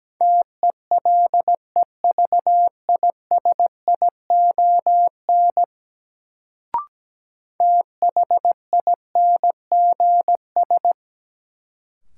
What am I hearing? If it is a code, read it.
Morse code, heard as a keyed tone, about 17 wpm, 700 Hz. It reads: TELEVISION THINGS
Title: Sentences from Top 1000 Words - Review Part 4 - New Word Drill - 17wpm